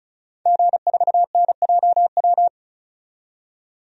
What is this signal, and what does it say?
Morse code, heard as a keyed tone, about 36 wpm, 700 Hz. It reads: G4NJW
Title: DXCC-100 Call Signs - 5 Characters - 35wpm